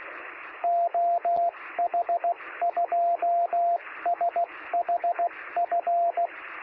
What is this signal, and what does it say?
Morse code, heard as a keyed tone, about 16 wpm, 690 Hz, with radio noise. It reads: OH2SHF
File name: sm7lcb / 2024z oh2shf-b 23cm